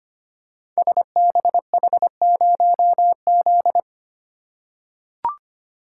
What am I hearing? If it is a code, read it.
Morse code, heard as a keyed tone, about 25 wpm, 700 Hz. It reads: SBH0Z
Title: Code Groups - 5 Letter:Numbers - 25wpm